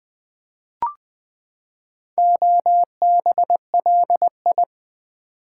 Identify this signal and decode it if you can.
Morse code, heard as a keyed tone, about 20 wpm, 700 Hz. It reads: OBLI
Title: Build A Word - 4 Letters At A Time - 10-Letter Words - 20wpm